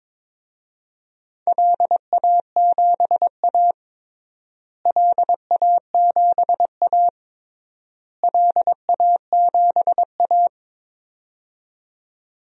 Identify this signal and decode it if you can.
Morse code, heard as a keyed tone, about 22 wpm, 700 Hz. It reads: LA7A LA7A LA7A
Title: DXCC-100 Call Signs - 4 Characters - Repeated 3x - 22wpm